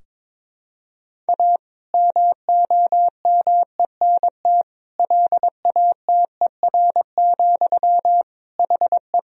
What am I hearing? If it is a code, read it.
Morse code, heard as a keyed tone, about 22 wpm, 700 Hz. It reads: A MOMENT LATER, HE
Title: Sentences from Top 1000 Words - Review Part 4 - 22wpm